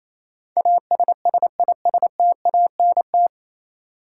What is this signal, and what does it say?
Morse code, heard as a keyed tone, about 28 wpm, 700 Hz. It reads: ASSISTANT
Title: Practice QSO - Exchange 3 - More Info - 28wpm